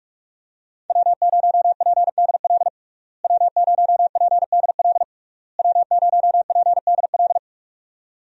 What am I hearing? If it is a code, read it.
Morse code, heard as a keyed tone, about 46 wpm, 700 Hz. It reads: W0PDL W0PDL W0PDL
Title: USA Call Signs - 5 Characters - Repeated 3x - 45wpm